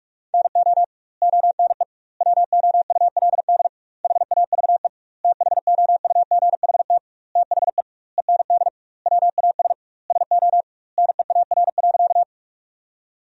Morse code, heard as a keyed tone, about 46 wpm, 700 Hz. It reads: NO ONE WOULD HAVE THOUGHT THE END WAS SO NEAR.
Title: 2x Word-Spacing - Sentences from Top 300 Words - 45wpm